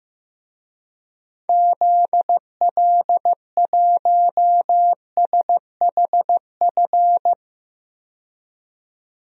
Morse code, heard as a keyed tone, about 15 wpm, 700 Hz. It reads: ZL1SHF
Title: DXCC-100 Call Signs - 6 Characters - 15wpm